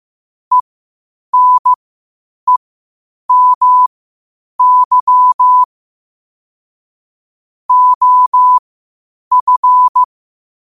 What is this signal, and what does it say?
Morse code, heard as a keyed tone, about 15 wpm, spaced as for 8 wpm, 1000 Hz. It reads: ENEMY OF
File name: qotdfiles / 8wpm